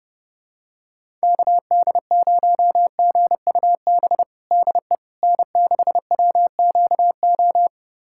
Morse code, heard as a keyed tone, about 32 wpm, 700 Hz. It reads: KD0GUB DE N6WQO
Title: USA Callsign DE Callsign 30wpm